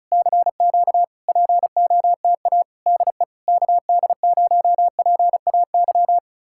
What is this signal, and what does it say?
Morse code, heard as a keyed tone, about 40 wpm, 700 Hz. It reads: CQ POTA DE KD0PAY